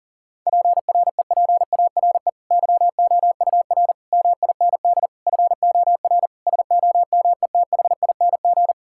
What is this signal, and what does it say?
Morse code, heard as a keyed tone, about 37 wpm, 700 Hz. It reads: PREPARE YOUR MIND FOR SOMETHING
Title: Sentences from Top 1000 Words - Review Part 6 - 40wpm